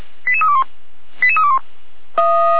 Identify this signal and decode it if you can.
Five-tone selective call (ZVEI), heard as quick runs of five tones, tones of about 80 ms. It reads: 88321 88321
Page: SDR/ZVEI